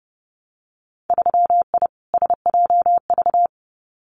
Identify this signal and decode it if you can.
Morse code, heard as a keyed tone, about 30 wpm, 700 Hz. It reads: 3I SJV
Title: Winter Field Day Exchanges Encore 30wpm